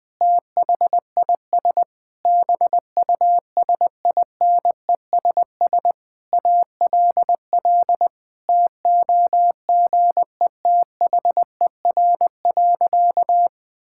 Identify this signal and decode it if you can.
Morse code, heard as a keyed tone, about 20 wpm, 700 Hz. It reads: THIS BUSINESS ALL TOGETHER.
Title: Sentences from Top 1000 Words - Review Part 6 - 20wpm